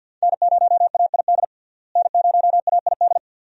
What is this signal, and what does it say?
Morse code, heard as a keyed tone, about 52 wpm, 700 Hz. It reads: N0AID N0AID
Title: USA Call Signs - 5 Characters - Repeated 2x - 50wpm